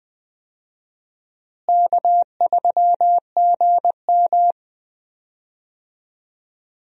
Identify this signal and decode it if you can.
Morse code, heard as a keyed tone, about 20 wpm, 700 Hz. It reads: K3GM
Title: USA Call Signs - 4 Characters - Repeated 3x - 20wpm